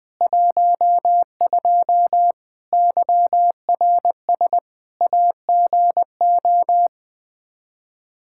Morse code, heard as a keyed tone, about 20 wpm, 700 Hz. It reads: 12 YRS AGO
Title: Practice QSO - Exchange 3 - More Info - 20wpm